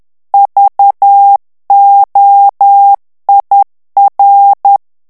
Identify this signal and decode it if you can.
Morse code, heard as a keyed tone, about 10 wpm, 800 Hz. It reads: VOIR
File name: morse mp3 / FRANCAIS TEXTE CLAIR 2 VITESSE 10 MOTS MINUTES